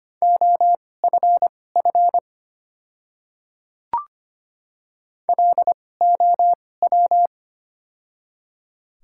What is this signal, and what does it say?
Morse code, heard as a keyed tone, about 25 wpm, spaced as for 18 wpm, 700 Hz. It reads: OFF LOW
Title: 2x Character Spacing - Top 200 Words - 25wpm